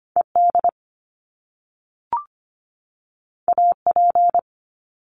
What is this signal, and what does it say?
Morse code, heard as a keyed tone, about 25 wpm, 700 Hz. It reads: ED AP
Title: Top Two Letter Combinations in Words 25wpm